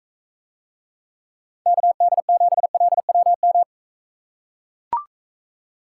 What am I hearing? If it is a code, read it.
Morse code, heard as a keyed tone, about 45 wpm, 700 Hz. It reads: KD7LWM